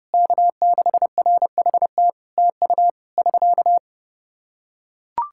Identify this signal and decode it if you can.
Morse code, heard as a keyed tone, about 30 wpm, 700 Hz. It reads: K6RHT TU <SK>
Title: Practice QSO - Exchange 4 - Ending - 30wpm